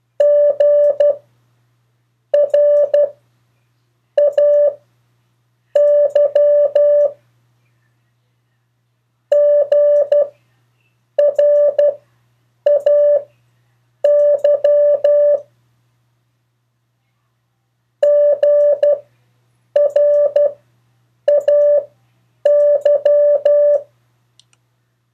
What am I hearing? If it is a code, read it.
Morse code, heard as a keyed tone, about 12 wpm, spaced as for 7 wpm, 580 Hz. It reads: GRAY GRAY GRAY
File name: 2of4-words-1